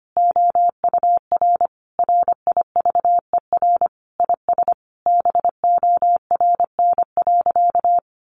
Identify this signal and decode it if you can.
Morse code, heard as a keyed tone, about 25 wpm, 700 Hz. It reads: OUR RIVER IS BORN.
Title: Sentences from Top 1200 Words 25wpm